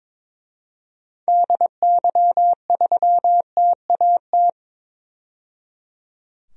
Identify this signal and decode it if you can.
Morse code, heard as a keyed tone, about 22 wpm, 700 Hz. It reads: DY3TAT
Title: DXCC-100 Call Signs - 6 Characters - 22wpm